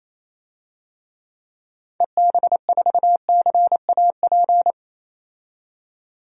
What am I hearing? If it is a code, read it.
Morse code, heard as a keyed tone, about 28 wpm, 700 Hz. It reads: EB4CAP
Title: DXCC-100 Call Signs - 6 Characters - 28wpm